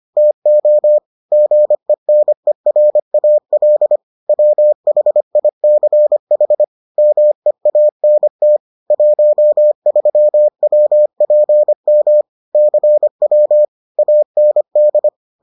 Morse code, heard as a keyed tone, about 25 wpm, 600 Hz. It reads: TO GENERAL WHICH MEANT 13WPM CW AND